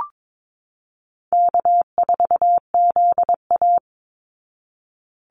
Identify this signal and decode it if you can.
Morse code, heard as a keyed tone, about 22 wpm, 700 Hz. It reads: K4ZA
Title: Easy Call Signs Repeated 2x - 22wpm